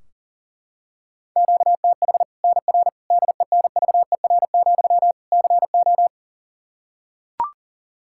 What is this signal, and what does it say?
Morse code, heard as a keyed tone, about 40 wpm, 700 Hz. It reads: QTH NR DENVER, CO
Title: K1USN SST QSO Practice 40wpm